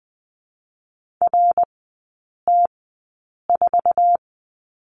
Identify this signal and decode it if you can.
Morse code, heard as a keyed tone, about 20 wpm, spaced as for 8 wpm, 700 Hz. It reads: RT4
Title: 053 Lesson 10e - Letter r - Three Characters 20wpm